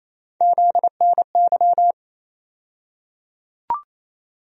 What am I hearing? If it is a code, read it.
Morse code, heard as a keyed tone, about 28 wpm, 700 Hz. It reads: ZNY
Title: USA Call Sign Postfixes - 3 Letters 28wpm